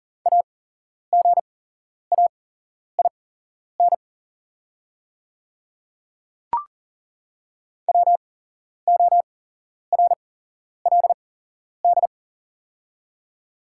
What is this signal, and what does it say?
Morse code, heard as a keyed tone, about 40 wpm, spaced as for 11 wpm, 700 Hz. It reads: AGAIN WORLD
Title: 8x Character Spacing - Top 200 Words - 40wpm